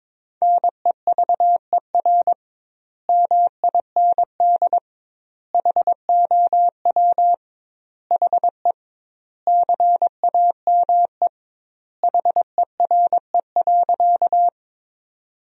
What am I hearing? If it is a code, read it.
Morse code, heard as a keyed tone, about 22 wpm, 700 Hz. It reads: NEVER MIND HOW HE CAME HERE.
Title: 2x Word-Spacing - Sentences from Top 500 Words - 22wpm